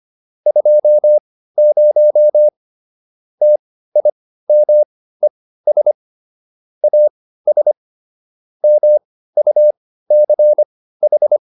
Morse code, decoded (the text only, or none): 20 TIMES AS MUCH